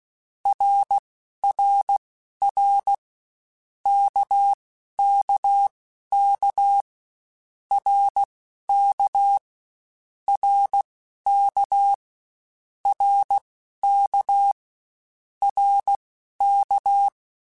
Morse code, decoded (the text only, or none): RRR KKK RK RK RK RK